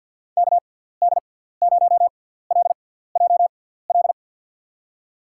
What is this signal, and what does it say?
Morse code, heard as a keyed tone, about 51 wpm, spaced as for 17 wpm, 700 Hz. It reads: KD0LJL
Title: USA Call Signs - 6 Characters - 6x Character Spacing - 50wpm